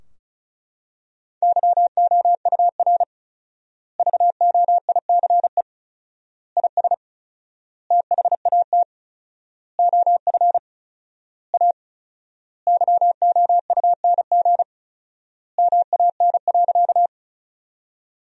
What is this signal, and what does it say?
Morse code, heard as a keyed tone, about 36 wpm, 700 Hz. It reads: YOUR VOICE IS THAT OF A YOUNG MAN.
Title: 4x Word-Spacing - Sentences from Top 500 Words - 35wpm